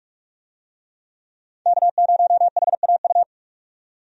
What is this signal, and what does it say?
Morse code, heard as a keyed tone, about 47 wpm, 700 Hz. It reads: K0HAU
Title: USA Call Signs - 5 Characters - 45wpm